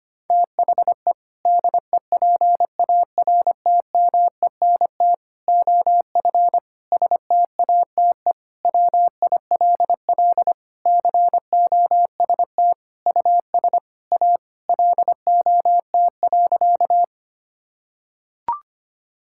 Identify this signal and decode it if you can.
Morse code, heard as a keyed tone, about 25 wpm, 700 Hz. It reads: THE DEPARTMENT OF STATE WILL COST US A LOT.
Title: Sentences from Top 1000 Words - Review Part 5 - 25wpm